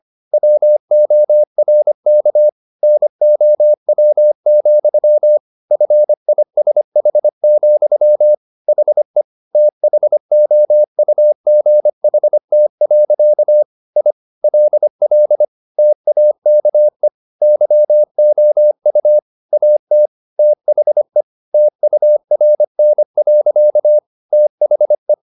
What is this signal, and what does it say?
Morse code, heard as a keyed tone, about 25 wpm, 600 Hz. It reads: WORK NOW, FISH, HE THOUGHT. I LL TAKE YOU AT THE TURN. THE